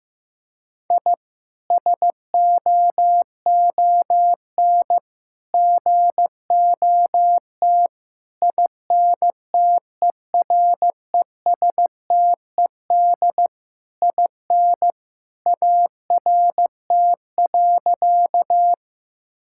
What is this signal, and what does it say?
Morse code, heard as a keyed tone, about 15 wpm, 700 Hz. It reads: I SOON GOT INTERESTED IN ART.
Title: Sentences from Top 1000 Words - Review Part 6 - 15wpm